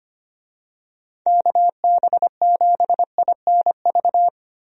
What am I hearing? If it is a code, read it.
Morse code, heard as a keyed tone, about 25 wpm, 700 Hz. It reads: KB7INV
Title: USA Call Signs - 6 Characters - Repeated 2x - 25wpm